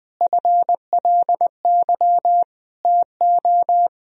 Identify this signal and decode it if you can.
Morse code, heard as a keyed tone, about 20 wpm, 700 Hz. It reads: FLY TO
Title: Sentences from Top 1000 Words - Review Part 2 - 20wpm